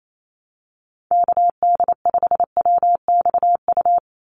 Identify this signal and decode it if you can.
Morse code, heard as a keyed tone, about 28 wpm, 700 Hz. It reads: KD5WXU
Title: USA Call Signs 28wpm